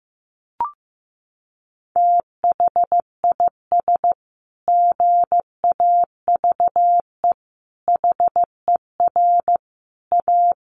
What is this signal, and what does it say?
Morse code, heard as a keyed tone, about 15 wpm, 700 Hz. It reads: THIS GAVE HER A
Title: Sentences from Top 1400 Words 15wpm